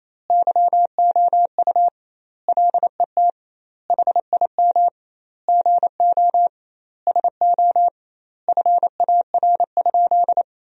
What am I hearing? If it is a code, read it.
Morse code, heard as a keyed tone, about 28 wpm, 700 Hz. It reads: YOU LET HIM GO SO FAR?
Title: 2x Word-Spacing - Sentences from Top 300 Words - 28wpm